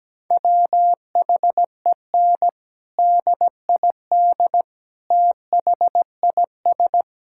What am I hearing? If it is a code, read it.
Morse code, heard as a keyed tone, about 17 wpm, 700 Hz. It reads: WHEN DID THIS
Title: Sentences from Top 1000 Words - Review Part 6 - 17wpm